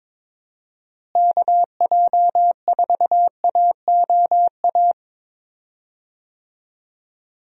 Morse code, heard as a keyed tone, about 22 wpm, 700 Hz. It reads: KJ4AOA
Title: USA Call Signs - 6 Characters - Repeated 2x - 22wpm